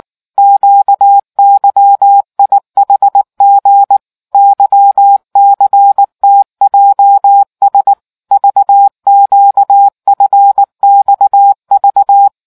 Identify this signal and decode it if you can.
Morse code, heard as a keyed tone, about 19 wpm, 800 Hz. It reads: QYIHG YCTJS VQFXV